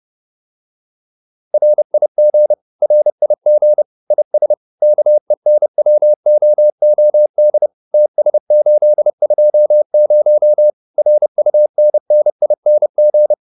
Morse code, heard as a keyed tone, about 30 wpm, 600 Hz. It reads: RIG RIG IS KENWOOD TS820 RUNNING